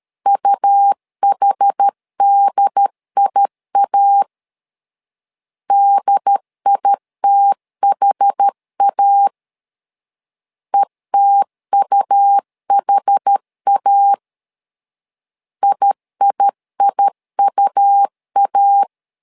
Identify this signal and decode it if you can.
Morse code, heard as a keyed tone, about 13 wpm, 800 Hz. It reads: UHDIA DITHA ETUHA IIIUA
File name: LEZIONI / 2 UHDN / 2 UHD 12wpm 1540